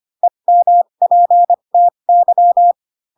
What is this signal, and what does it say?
Morse code, heard as a keyed tone, about 25 wpm, spaced as for 22 wpm, 700 Hz. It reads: EMPTY